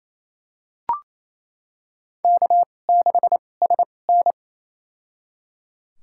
Morse code, heard as a keyed tone, about 28 wpm, spaced as for 20 wpm, 700 Hz. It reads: K6SN
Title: USA Call Signs - 4 Characters - 2x Character Spacing - 28wpm